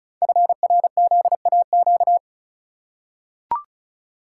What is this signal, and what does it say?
Morse code, heard as a keyed tone, about 34 wpm, 700 Hz. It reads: FRZAQ
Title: Code Groups - 5 Letters - 35wpm